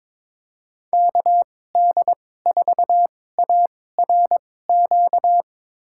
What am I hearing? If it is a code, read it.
Morse code, heard as a keyed tone, about 22 wpm, spaced as for 16 wpm, 700 Hz. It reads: KD4ARQ
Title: USA Call Signs - 6 Characters - 2x Character Spacing - 22wpm